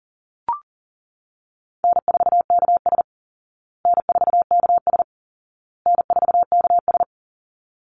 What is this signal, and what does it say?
Morse code, heard as a keyed tone, about 40 wpm, 700 Hz. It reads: N4KS N4KS N4KS
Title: Easy Call Signs Repeated 3x - 40wpm